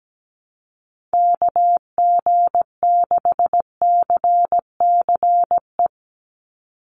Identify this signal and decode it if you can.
Morse code, heard as a keyed tone, about 17 wpm, 700 Hz. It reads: KG6CCE
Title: USA Call Signs 17wpm